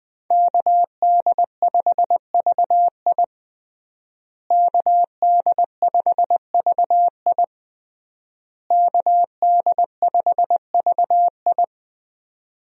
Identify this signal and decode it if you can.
Morse code, heard as a keyed tone, about 21 wpm, 700 Hz. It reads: KD5VI KD5VI KD5VI
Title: USA Call Signs - 5 Characters - Repeated 3x - 20wpm